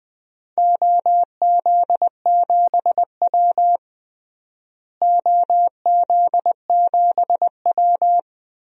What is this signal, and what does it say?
Morse code, heard as a keyed tone, about 20 wpm, 700 Hz. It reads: OZ7W OZ7W
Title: DXCC-100 Call Signs - 4 Characters - Repeated 3x - 20wpm